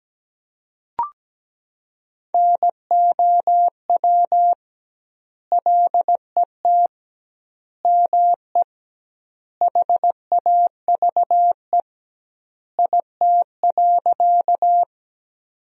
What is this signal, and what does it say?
Morse code, heard as a keyed tone, about 17 wpm, 700 Hz. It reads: NOW LET ME HAVE IT.
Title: 2x Word-Spacing - Sentences from Top 300 Words - 17wpm